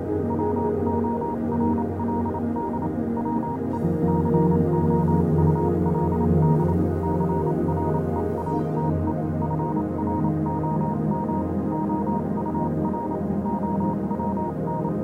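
Morse code, heard as a keyed tone, about 28 wpm, spaced as for 25 wpm, 940 Hz, with noise. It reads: WORLD WE COULD CREATE FROM OUR OWN
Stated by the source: Morse code version of a fan-made pony music track.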